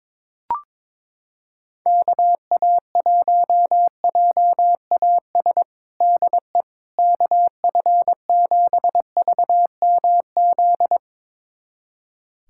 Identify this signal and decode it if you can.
Morse code, heard as a keyed tone, about 22 wpm, 700 Hz. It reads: KA1JAS DE KF7VMZ
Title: USA Callsign DE Callsign - 6 Characters - 22wpm